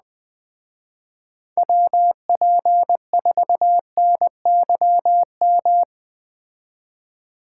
Morse code, heard as a keyed tone, about 19 wpm, 700 Hz. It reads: WP4NYM